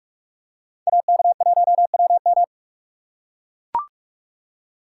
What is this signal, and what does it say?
Morse code, heard as a keyed tone, about 44 wpm, 700 Hz. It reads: AK1WM